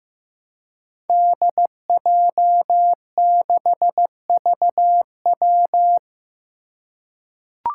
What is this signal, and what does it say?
Morse code, heard as a keyed tone, about 15 wpm, 700 Hz. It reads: DJ6VW
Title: Five Challenging Letter-Number Combinations 15wpm